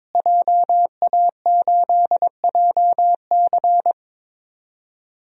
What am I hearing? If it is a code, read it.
Morse code, heard as a keyed tone, about 23 wpm, 700 Hz. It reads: JA8JC
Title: DXCC-100 Call Signs - 5 Characters - Repeated 2x - 22wpm